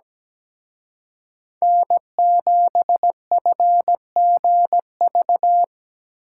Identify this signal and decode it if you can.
Morse code, heard as a keyed tone, about 20 wpm, 700 Hz. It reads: N7FGV
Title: Speed Racing - USA Call Signs 17wpm